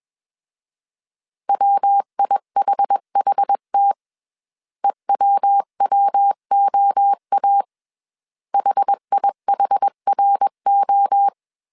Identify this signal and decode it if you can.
Morse code, heard as a keyed tone, about 21 wpm, 800 Hz. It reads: WIHHT EWWOA HIHRO
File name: CW / DET / 1 3 1500C S B 20wpm 1754